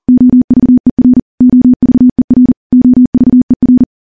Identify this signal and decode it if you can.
Morse code, heard as a keyed tone, about 40 wpm, 250 Hz. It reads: OVER OVER OVER
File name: morse / morse-training-40WPM